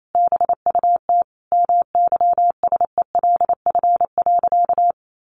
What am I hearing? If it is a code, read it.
Morse code, heard as a keyed tone, about 28 wpm, 700 Hz. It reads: BUT MYSELF.
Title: Sentences from Top 1300 Words 28wpm